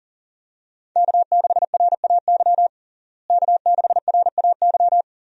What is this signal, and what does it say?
Morse code, heard as a keyed tone, about 37 wpm, 700 Hz. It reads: K6RAY K6RAY
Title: USA Call Signs - 5 Characters - Repeated 2x - 40wpm